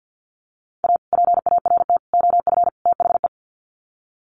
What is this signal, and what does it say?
Morse code, heard as a keyed tone, about 50 wpm, 700 Hz. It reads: A PART OF THE
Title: Sets of 4 Words 50wpm